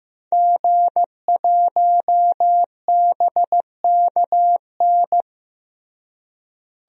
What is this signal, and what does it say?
Morse code, heard as a keyed tone, about 15 wpm, 700 Hz. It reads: G1BKN
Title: DXCC-100 Call Signs - 5 Characters - Repeated 3x - 15wpm